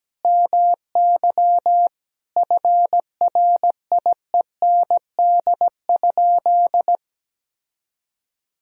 Morse code, heard as a keyed tone, about 17 wpm, 700 Hz. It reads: MY FRIEND?
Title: Sentences from Top 1000 Words - Review Part 3 - 17wpm